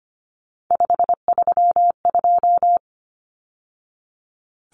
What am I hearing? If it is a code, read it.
Morse code, heard as a keyed tone, about 25 wpm, 700 Hz. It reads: 532
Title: Numbers - 3-Digit 25wpm